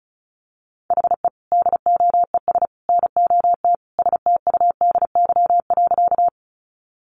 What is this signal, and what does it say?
Morse code, heard as a keyed tone, about 34 wpm, 700 Hz. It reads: HE DOES NOT STUDY.
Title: Additional Easy Phrases 35wpm